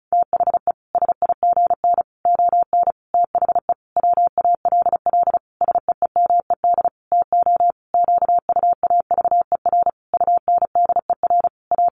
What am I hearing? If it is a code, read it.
Morse code, heard as a keyed tone, about 34 wpm, 700 Hz. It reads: THE SIGN ON THE WALL SEEMED TO QUAVER UNDER A
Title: A Sound Of Thunder 35wpm